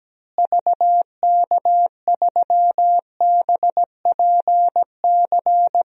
Morse code, heard as a keyed tone, about 17 wpm, 700 Hz. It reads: VK3BPC
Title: DXCC-100 Call Signs - 6 Characters - Repeated 3x - 17wpm